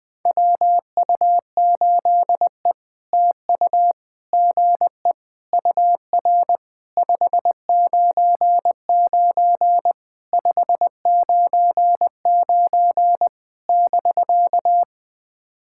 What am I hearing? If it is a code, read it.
Morse code, heard as a keyed tone, about 19 wpm, 700 Hz. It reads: WU8E TU GE UR 599 599 <BK>
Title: POTA QSO Practice - 20wpm